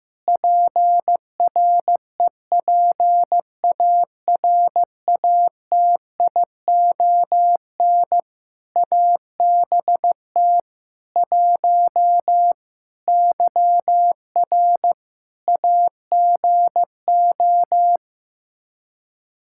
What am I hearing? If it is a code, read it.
Morse code, heard as a keyed tone, about 15 wpm, 700 Hz. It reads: PREPARATION ABT 1 YR AGO
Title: Practice QSO - Exchange 3 - More Info - 15wpm